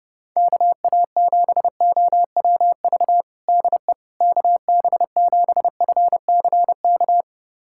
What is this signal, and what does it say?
Morse code, heard as a keyed tone, about 32 wpm, 700 Hz. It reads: KA7OWV DE KB7FCK
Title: USA Callsign DE Callsign - 6 Characters - 30wpm